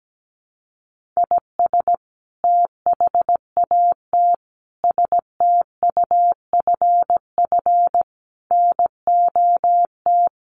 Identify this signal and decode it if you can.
Morse code, heard as a keyed tone, about 17 wpm, 700 Hz. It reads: IS THAT STUFF NOT
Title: Sentences from Top 1400 Words 17wpm